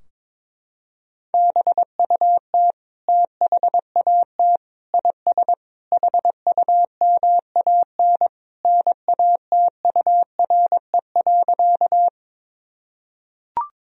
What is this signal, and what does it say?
Morse code, heard as a keyed tone, about 21 wpm, 700 Hz. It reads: BUT THAT IS HUMAN NATURE.
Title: Sentences from Top 1000 Words - Review Part 3 - 22wpm